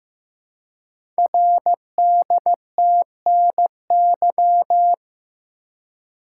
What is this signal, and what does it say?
Morse code, heard as a keyed tone, about 15 wpm, 700 Hz. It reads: RDTNY
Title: Code Groups - 5 Letters - 15wpm